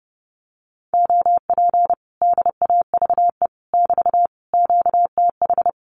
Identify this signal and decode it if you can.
Morse code, heard as a keyed tone, about 30 wpm, 700 Hz. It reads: OP DAVE <BT> QTH
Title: Common QSO Phrases Encore 30wpm